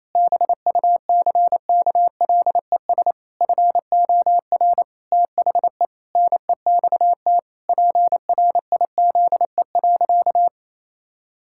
Morse code, heard as a keyed tone, about 28 wpm, 700 Hz. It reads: BUCKLES FOR THE NEXT PRIZE.
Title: Pangram Challenge v2 - 28wpm